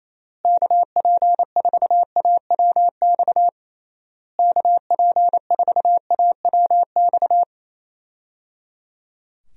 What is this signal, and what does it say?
Morse code, heard as a keyed tone, about 28 wpm, 700 Hz. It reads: KP4AWX KP4AWX
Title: DXCC-100 Call Signs - 6 Characters - Repeated 3x - 28wpm